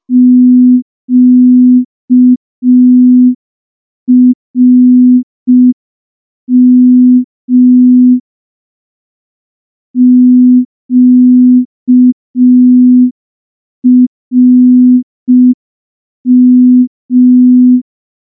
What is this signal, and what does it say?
Morse code, heard as a keyed tone, about 5 wpm, 250 Hz. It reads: QRM QRM
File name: morse / morse-training-05WPM